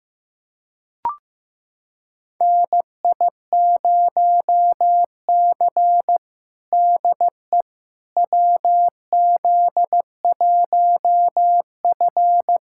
Morse code, heard as a keyed tone, about 16 wpm, 700 Hz. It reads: NI0C DE WZ1F
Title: USA Callsign DE Callsign - 4 Characters - 15wpm